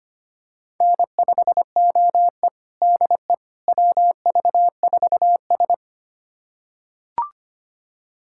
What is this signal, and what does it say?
Morse code, heard as a keyed tone, about 25 wpm, 700 Hz. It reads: N5OE DE WV4S
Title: USA Callsign DE Callsign - 4 Characters - 25wpm